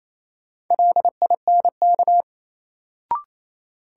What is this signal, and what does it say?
Morse code, heard as a keyed tone, about 28 wpm, 700 Hz. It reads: LINK
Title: Sentences from Top 1000 Words - Review Part 7 - New Word Drill - 28wpm